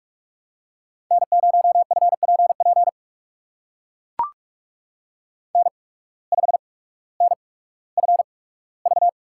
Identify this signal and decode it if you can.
Morse code, heard as a keyed tone, about 42 wpm, spaced as for 19 wpm, 700 Hz. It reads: N0FPP N5NFV
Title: USA Call Signs - 5 Characters - 8x Character Spacing - 45wpm